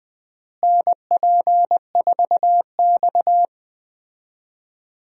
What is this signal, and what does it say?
Morse code, heard as a keyed tone, about 20 wpm, 700 Hz. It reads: NP4X
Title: USA Call Signs - 4 Characters - Repeated 3x - 20wpm